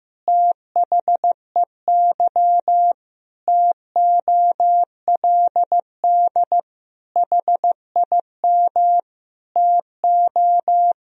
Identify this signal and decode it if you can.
Morse code, heard as a keyed tone, about 15 wpm, 700 Hz. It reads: THEY TOLD HIM TO
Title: Sentences from Top 1000 Words - Review Part 3 - 15wpm